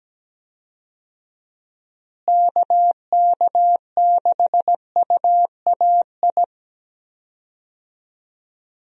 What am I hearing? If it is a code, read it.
Morse code, heard as a keyed tone, about 17 wpm, 700 Hz. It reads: KK6UAI
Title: USA Call Signs - 6 Characters - Repeated 3x - 17wpm